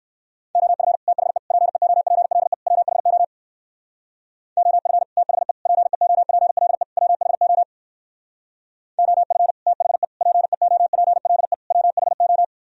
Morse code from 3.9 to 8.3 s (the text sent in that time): OF THE PEOPLE WHO